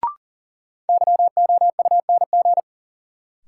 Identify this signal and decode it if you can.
Morse code, heard as a keyed tone, about 40 wpm, 700 Hz. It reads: YOUNG
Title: Sentences from Top 400 Words - New Word Drill - 40wpm